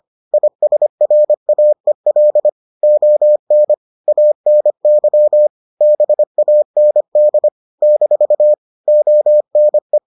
Morse code, heard as a keyed tone, about 25 wpm, 600 Hz. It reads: ISRAEL ON ANY BAND - ONE